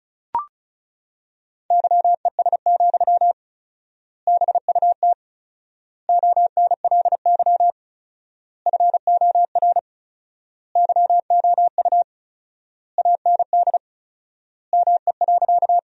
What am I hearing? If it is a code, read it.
Morse code, heard as a keyed tone, about 34 wpm, 700 Hz. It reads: YES, BUT ONLY FOR YOU AND ME.
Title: 4x Word-Spacing - Sentences from Top 500 Words - 35wpm